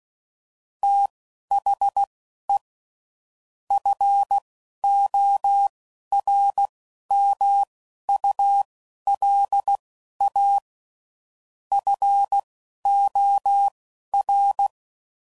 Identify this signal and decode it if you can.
Morse code, heard as a keyed tone, about 16 wpm, spaced as for 11 wpm, 800 Hz. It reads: THE FORMULA FOR
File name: CW Code Groups @10WPM / Technical Subjects / XL